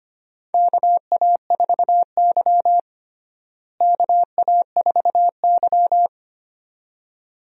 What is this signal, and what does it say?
Morse code, heard as a keyed tone, about 25 wpm, 700 Hz. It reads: KA4Y KA4Y
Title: USA Call Signs - 4 Characters - Repeated 2x - 25wpm